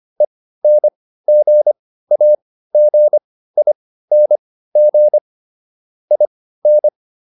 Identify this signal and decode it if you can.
Morse code, heard as a keyed tone, about 25 wpm, spaced as for 15 wpm, 600 Hz. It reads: ENGAGING IN